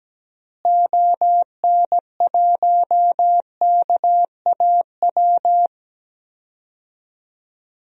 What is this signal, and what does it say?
Morse code, heard as a keyed tone, about 17 wpm, 700 Hz. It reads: ON1KAW
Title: DXCC-100 Call Signs - 6 Characters - Repeated 3x - 17wpm